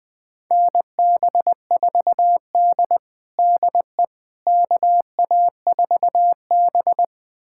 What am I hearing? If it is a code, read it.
Morse code, heard as a keyed tone, about 20 wpm, 700 Hz. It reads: NB4D DE KA4B
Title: USA Callsign DE Callsign - 4 Characters - 20wpm